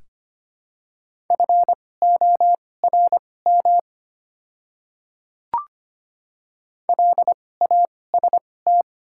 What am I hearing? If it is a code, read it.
Morse code, heard as a keyed tone, about 25 wpm, spaced as for 18 wpm, 700 Hz. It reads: FORM LAST
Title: 2x Character Spacing - Top 300 Words - 25wpm